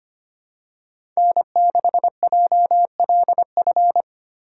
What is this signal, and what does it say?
Morse code, heard as a keyed tone, about 25 wpm, 700 Hz. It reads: N6JLF
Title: USA Call Signs - 5 Characters - 3x Character Spacing - 25wpm